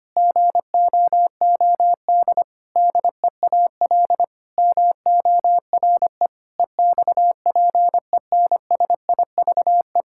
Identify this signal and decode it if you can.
Morse code, heard as a keyed tone, about 25 wpm, 700 Hz. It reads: GOOD DEAL MORE EXPENSIVE
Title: Sentences from Top 1000 Words - Review Part 6 - 25wpm